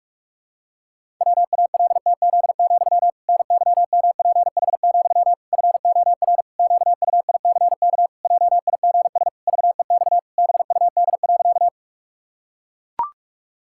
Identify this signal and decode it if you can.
Morse code, heard as a keyed tone, about 41 wpm, 700 Hz. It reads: WALTZ, NYMPH, FOR QUICK JIGS VEX BUD.
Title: Pangram Challenge v2 - 45wpm